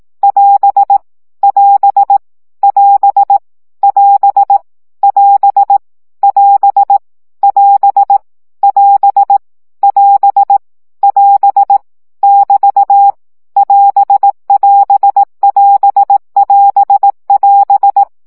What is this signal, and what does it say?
Morse code, heard as a keyed tone, about 18 wpm, 800 Hz. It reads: <AS> <AS> <AS> <AS> <AS> <AS> <AS> <AS> <AS> <AS> <BT> <AS><AS><AS><AS><AS>
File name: CW gyakorlas as 15 baud